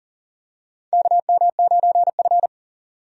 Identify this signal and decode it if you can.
Morse code, heard as a keyed tone, about 39 wpm, 700 Hz. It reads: KM9F
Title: USA Call Signs - 4 Characters - 40wpm